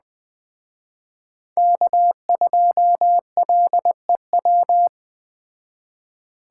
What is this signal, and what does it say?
Morse code, heard as a keyed tone, about 20 wpm, 700 Hz. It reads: K2LEW
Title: Speed Racing - USA Call Signs 17wpm